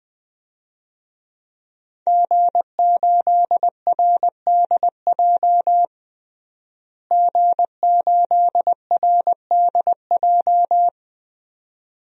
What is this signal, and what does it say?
Morse code, heard as a keyed tone, about 20 wpm, 700 Hz. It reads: G8RDJ G8RDJ
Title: DXCC-100 Call Signs - 5 Characters - Repeated 2x - 20wpm